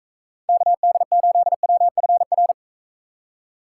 Morse code, heard as a keyed tone, about 41 wpm, 700 Hz. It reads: KD8WFR